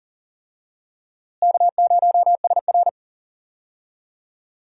Morse code, heard as a keyed tone, about 40 wpm, 700 Hz. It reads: K0SR
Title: USA Call Signs - 4 Characters - Repeated 2x - 40wpm